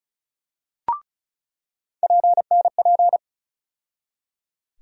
Morse code, heard as a keyed tone, about 36 wpm, 700 Hz. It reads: PNP
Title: USA Call Sign Postfixes - 3 Letters 35wpm